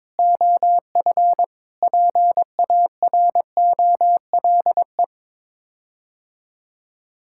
Morse code, heard as a keyed tone, about 21 wpm, 700 Hz. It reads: OF PAROLE
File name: practice / 22wpm / Sets of 8 Words - 22wpm